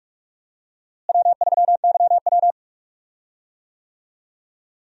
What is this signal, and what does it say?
Morse code, heard as a keyed tone, about 44 wpm, 700 Hz. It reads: W3YW
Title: USA Call Signs - 4 Characters - Repeated 3x - 45wpm